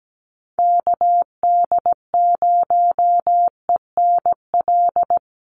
Morse code, heard as a keyed tone, about 17 wpm, 700 Hz. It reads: KD0ENL
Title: USA Call Signs Repeated 3x 17wpm